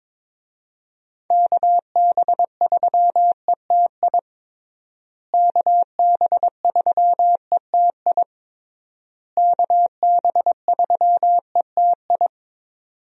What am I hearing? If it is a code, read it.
Morse code, heard as a keyed tone, about 22 wpm, 700 Hz. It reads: KB3ETI KB3ETI KB3ETI
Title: USA Call Signs - 6 Characters - Repeated 3x - 22wpm